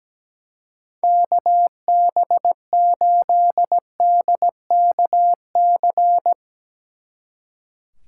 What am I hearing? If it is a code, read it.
Morse code, heard as a keyed tone, about 19 wpm, 700 Hz. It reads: KB8DKC
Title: Speed Racing - USA 2x3 Call Signs 17wpm